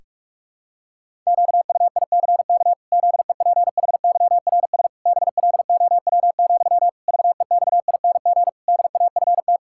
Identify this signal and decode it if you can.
Morse code, heard as a keyed tone, about 46 wpm, 700 Hz. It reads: QUICK ZEPHYRS BLOW, VEXING DAFT
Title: Pangram Challenge v2 - 45wpm